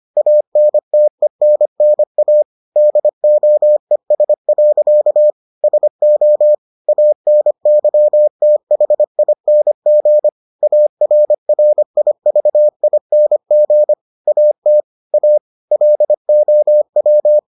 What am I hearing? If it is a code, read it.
Morse code, heard as a keyed tone, about 25 wpm, 600 Hz. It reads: ANTENNA DOES. SO ANYTHING ARRIVING AT A LOW